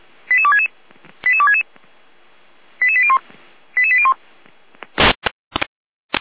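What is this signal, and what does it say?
Five-tone selective call (ZVEI), heard as quick runs of five tones, tones of about 70 ms. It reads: 88266 88266 88801 88801
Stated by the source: Five-tone ZVEI selective call received on SDR.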